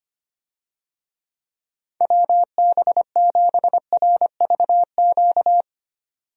Morse code, heard as a keyed tone, about 25 wpm, 700 Hz. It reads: WB7RVQ
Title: USA Call Signs - 6 Characters - 25wpm